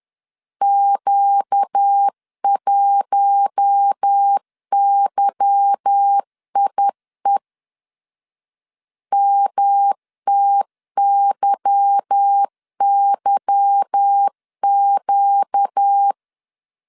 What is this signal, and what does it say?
Morse code, heard as a keyed tone, about 10 wpm, 800 Hz. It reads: Q1YIE MTYYQ
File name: LEZIONI / 8 YJ45 / 8 Y 10wpm 2615